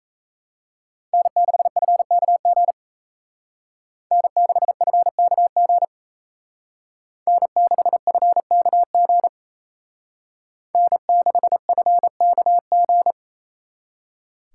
Morse code, heard as a keyed tone, about 34 wpm, 700 Hz. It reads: N6FKG N6FKG N6FKG N6FKG
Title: Speed Racing - USA 1x3 Call Signs 28wpm